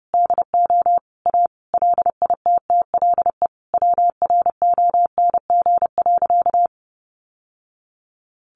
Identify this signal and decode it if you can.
Morse code, heard as a keyed tone, about 28 wpm, 700 Hz. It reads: DO A LITTLE WRONG.
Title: Sentences from Top 900 Words 30wpm